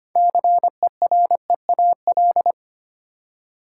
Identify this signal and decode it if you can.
Morse code, heard as a keyed tone, about 25 wpm, 690 Hz. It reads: CEREAL
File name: practice / 25wpm / Easy Words with Hints - 25wpm